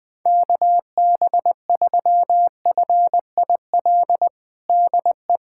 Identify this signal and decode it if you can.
Morse code, heard as a keyed tone, about 20 wpm, 700 Hz. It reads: KB3FIL DE
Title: USA Callsign DE Callsign - 6 Characters - 20wpm